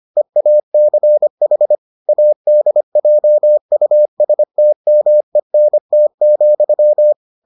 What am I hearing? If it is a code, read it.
Morse code, heard as a keyed tone, about 25 wpm, 600 Hz. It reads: EACH ADJUSTMENT,